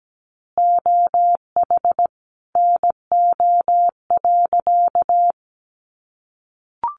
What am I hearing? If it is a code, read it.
Morse code, heard as a keyed tone, about 17 wpm, 700 Hz. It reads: OH NO.
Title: Additional Easy Phrases 17wpm